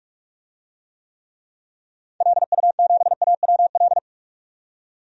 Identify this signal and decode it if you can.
Morse code, heard as a keyed tone, about 43 wpm, 700 Hz. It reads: LU7AWL
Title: DXCC-100 Call Signs - 6 Characters - 45wpm